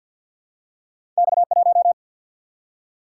Morse code, heard as a keyed tone, about 50 wpm, 700 Hz. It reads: X 1
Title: Single-Letter Single-Number 50wpm